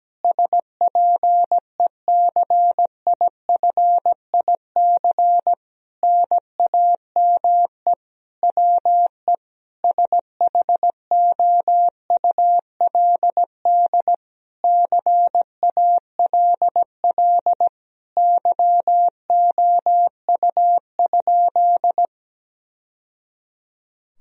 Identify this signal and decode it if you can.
Morse code, heard as a keyed tone, about 17 wpm, 700 Hz. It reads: SPECIFIC NAME WE SHOULD CALL YOU?
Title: Sentences from Top 1000 Words - Review Part 3 - 17wpm